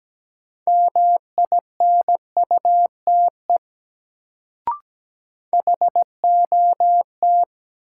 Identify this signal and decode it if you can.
Morse code, heard as a keyed tone, about 17 wpm, 700 Hz. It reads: MINUTE HOT
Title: Sentences from Top 500 Words - New Word Drill - 17wpm